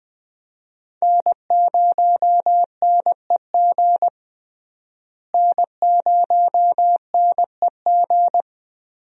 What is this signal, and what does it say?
Morse code, heard as a keyed tone, about 21 wpm, 700 Hz. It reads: N0NEG N0NEG
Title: USA Call Signs - 5 Characters - Repeated 3x - 20wpm